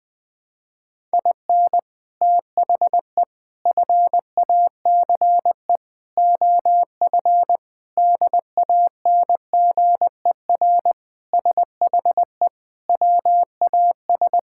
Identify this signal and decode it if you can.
Morse code, heard as a keyed tone, about 20 wpm, 700 Hz. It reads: IN THE FACE OF DANGER SHE WAS
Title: Sentences with 4-Word Phrases - 20wpm